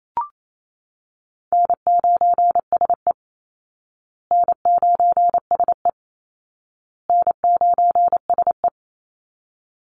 Morse code, heard as a keyed tone, about 28 wpm, 700 Hz. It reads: N9SE N9SE N9SE
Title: Easy Call Signs Repeated 3x - 28wpm